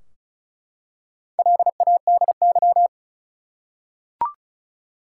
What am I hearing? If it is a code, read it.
Morse code, heard as a keyed tone, about 36 wpm, 700 Hz. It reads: LADY
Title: Four Letter Words - 35wpm